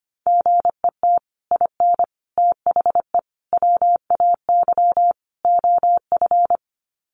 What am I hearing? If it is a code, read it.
Morse code, heard as a keyed tone, about 25 wpm, 700 Hz. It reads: GET IN THE WAY OF
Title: Sets of 5 Words Encore 25wpm